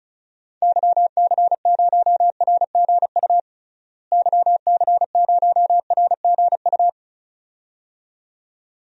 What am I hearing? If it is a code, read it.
Morse code, heard as a keyed tone, about 36 wpm, 700 Hz. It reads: YC0RGU YC0RGU
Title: DXCC-100 Call Signs - 6 Characters - Repeated 2x - 35wpm